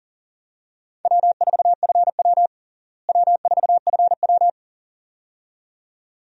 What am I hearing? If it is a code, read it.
Morse code, heard as a keyed tone, about 40 wpm, 700 Hz. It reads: W4FW W4FW
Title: USA Call Signs - 4 Characters - Repeated 2x - 40wpm